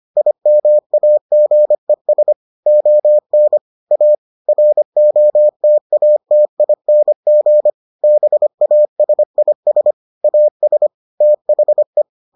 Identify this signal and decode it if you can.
Morse code, heard as a keyed tone, about 25 wpm, 600 Hz. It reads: IMAGES ON A ROTATING BASIS AS THE